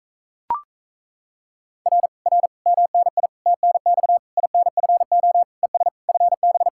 Morse code, heard as a keyed tone, about 37 wpm, 700 Hz. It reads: R R MNI TNX INFO ES FB